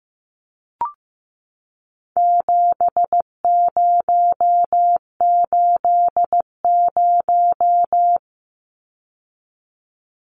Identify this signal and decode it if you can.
Morse code, heard as a keyed tone, about 15 wpm, 700 Hz. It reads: 7080
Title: Numbers - 4-Digit 15wpm